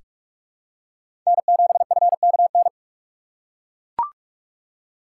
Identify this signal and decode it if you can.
Morse code, heard as a keyed tone, about 41 wpm, 690 Hz. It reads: N7FKN
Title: USA Call Signs - 6x Character Spacing - 45wpm